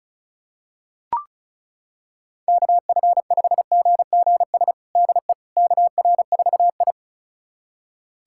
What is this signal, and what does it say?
Morse code, heard as a keyed tone, about 36 wpm, 700 Hz. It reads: KF5GGS DE KR4I
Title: USA Callsign DE Callsign 35wpm